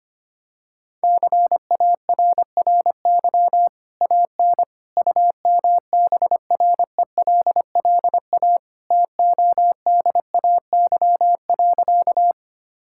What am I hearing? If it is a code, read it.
Morse code, heard as a keyed tone, about 25 wpm, 700 Hz. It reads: CARRY AN UMBRELLA TODAY.
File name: Easy Sentences with Hints - 25wpm